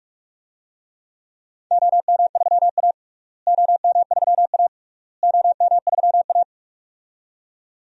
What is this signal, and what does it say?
Morse code, heard as a keyed tone, about 41 wpm, 690 Hz. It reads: OM3A OM3A OM3A
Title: DXCC-100 Call Signs - 4 Characters - Repeated 3x - 45wpm